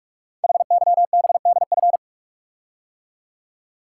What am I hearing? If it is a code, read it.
Morse code, heard as a keyed tone, about 44 wpm, 690 Hz. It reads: HYBDF
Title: Code Groups - 5 Letter:Numbers - 45wpm